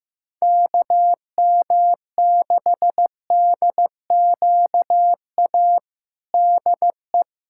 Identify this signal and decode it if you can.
Morse code, heard as a keyed tone, about 15 wpm, 700 Hz. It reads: KM6DQA DE
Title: USA Callsign DE Callsign 15wpm